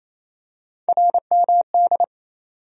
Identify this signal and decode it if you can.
Morse code, heard as a keyed tone, about 28 wpm, 700 Hz. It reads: RMD
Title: USA Call Sign Postfixes - 3 Letters 28wpm